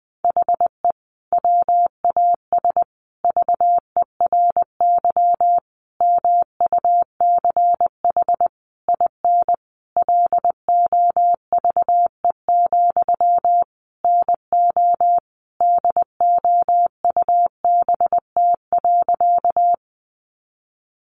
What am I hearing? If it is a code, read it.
Morse code, heard as a keyed tone, about 20 wpm, 700 Hz. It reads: HE WAS VERY MUCH IN LOVE, NO DOUBT.
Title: Sentences from Top 1500 Words 20wpm